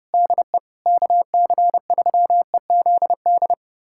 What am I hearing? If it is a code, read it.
Morse code, heard as a keyed tone, about 30 wpm, 700 Hz. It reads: DE KC3EZD
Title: USA Callsign DE Callsign 30wpm